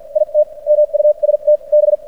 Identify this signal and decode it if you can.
Morse code, heard as a keyed tone, about 56 wpm, spaced as for 43 wpm, 600 Hz. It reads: IT MUST B
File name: archived / CRSnet